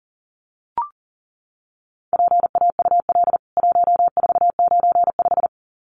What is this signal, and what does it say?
Morse code, heard as a keyed tone, about 40 wpm, 700 Hz. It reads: PAUL 1495
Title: CWT Exchange 40wpm